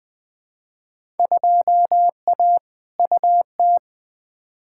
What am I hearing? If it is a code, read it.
Morse code, heard as a keyed tone, about 20 wpm, 700 Hz. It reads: 2A UT
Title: Field Day Exchanges - 20wpm